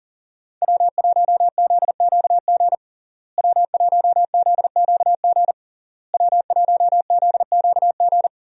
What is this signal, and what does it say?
Morse code, heard as a keyed tone, about 40 wpm, 700 Hz. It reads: W1ZQG W1ZQG W1ZQG
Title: USA Call Signs - 5 Characters - Repeated 3x - 40wpm